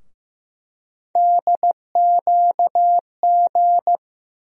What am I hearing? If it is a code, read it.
Morse code, heard as a keyed tone, about 15 wpm, 700 Hz. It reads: DQG
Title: USA Call Sign Postfixes - 3 Letters 15wpm